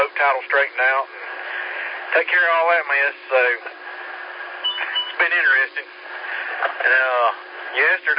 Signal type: Emergency Alert System (SAME) tones